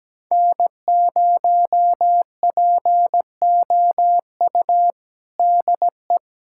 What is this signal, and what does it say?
Morse code, heard as a keyed tone, about 17 wpm, 700 Hz. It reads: N0POU DE
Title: USA Callsign DE Callsign - 5 Characters - 17wpm